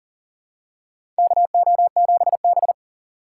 Morse code, heard as a keyed tone, about 40 wpm, 700 Hz. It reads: KO7B
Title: USA Call Signs - 3x Character Spacing - 40wpm